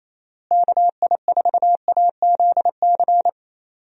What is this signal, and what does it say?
Morse code, heard as a keyed tone, about 27 wpm, 700 Hz. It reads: KI4AZC
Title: USA Call Signs - 6 Characters - Repeated 3x - 28wpm